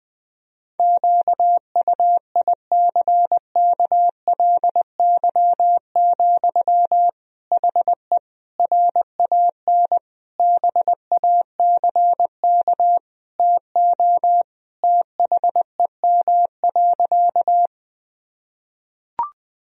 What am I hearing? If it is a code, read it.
Morse code, heard as a keyed tone, about 21 wpm, 700 Hz. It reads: QUICKLY, HE RAN BACK TO THEM.
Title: Sentences from Top 1000 Words - Review Part 3 - 20wpm